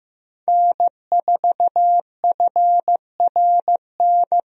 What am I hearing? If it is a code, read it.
Morse code, heard as a keyed tone, about 15 wpm, 700 Hz. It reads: N4FRN
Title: USA Call Signs - 5 Characters - Repeated 3x - 15wpm